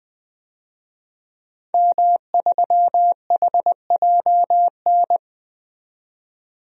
Morse code, heard as a keyed tone, about 20 wpm, 700 Hz. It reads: M3HJN
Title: DXCC-100 Call Signs - 5 Characters - 20wpm